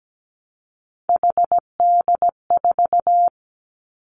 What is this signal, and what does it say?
Morse code, heard as a keyed tone, about 17 wpm, 700 Hz. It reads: HD4
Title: Three Challenging Letter-Number Combinations 17wpm